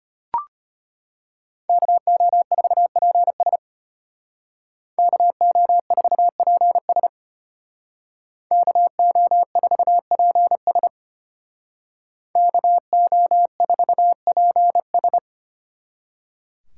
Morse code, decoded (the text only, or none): KO4PS KO4PS KO4PS KO4PS